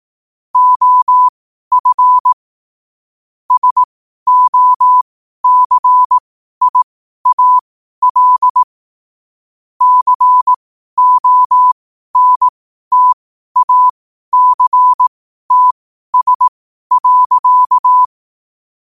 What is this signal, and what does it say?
Morse code, heard as a keyed tone, about 18 wpm, spaced as for 12 wpm, 1000 Hz. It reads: OF SOCIAL CONTACTS.
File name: qotdfiles / 12wpm